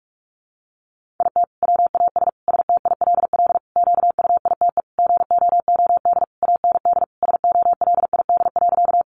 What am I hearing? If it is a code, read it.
Morse code, heard as a keyed tone, about 44 wpm, 700 Hz. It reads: IT WAS STILL QUITE GOOD AND SOLID.
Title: Sentences from Top 1500 Words 45wpm